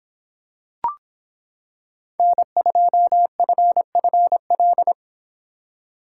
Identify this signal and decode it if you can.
Morse code, heard as a keyed tone, about 26 wpm, 700 Hz. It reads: N2FFL